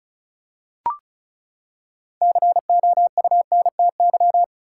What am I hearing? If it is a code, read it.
Morse code, heard as a keyed tone, about 34 wpm, 700 Hz. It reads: COUNTY
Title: Six Letter Words - 35wpm